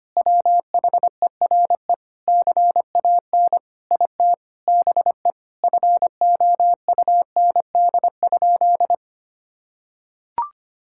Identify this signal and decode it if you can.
Morse code, heard as a keyed tone, about 25 wpm, 700 Hz. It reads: WHERE CAN IT BE FOUND?
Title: Sentences from Top 1600 Words 25wpm